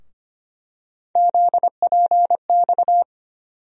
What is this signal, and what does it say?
Morse code, heard as a keyed tone, about 25 wpm, 700 Hz. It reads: ZPX
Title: USA Call Sign Postfixes - 3 Letters 25wpm